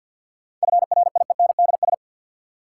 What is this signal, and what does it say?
Morse code, heard as a keyed tone, about 52 wpm, 700 Hz. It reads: FRIENDS
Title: Sentences from Top 1000 Words - Review Part 5 - New Word Drill - 50wpm